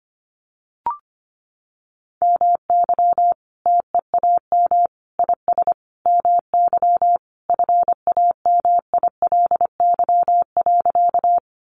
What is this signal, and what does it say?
Morse code, heard as a keyed tone, about 25 wpm, 700 Hz. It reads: MY TEAM IS MY FAMILY.
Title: Sentences from Top 1500 Words 25wpm